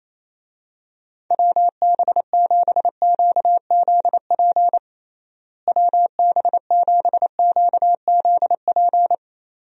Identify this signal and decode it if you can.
Morse code, heard as a keyed tone, about 28 wpm, 700 Hz. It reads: WB7QZP WB7QZP
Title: USA Call Signs - 6 Characters - Repeated 2x - 28wpm